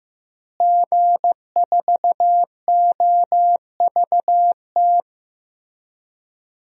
Morse code, heard as a keyed tone, about 15 wpm, 700 Hz. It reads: G4OVT
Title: DXCC-100 Call Signs - 5 Characters - Repeated 3x - 15wpm